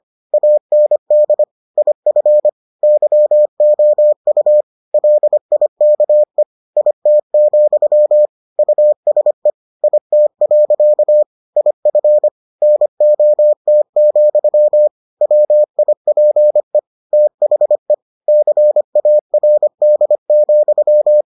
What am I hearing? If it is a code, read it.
Morse code, heard as a keyed tone, about 25 wpm, 600 Hz. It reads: AND IF YOU LIKE IT, USE IT. IF NOT, WIPE THE CARD,